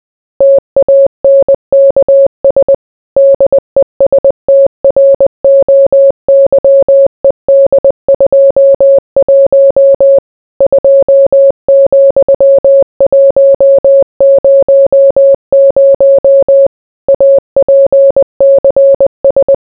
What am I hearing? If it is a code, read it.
Morse code, heard as a keyed tone, about 20 wpm, 550 Hz. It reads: TANKS DESTROYED 21 2,100 APCS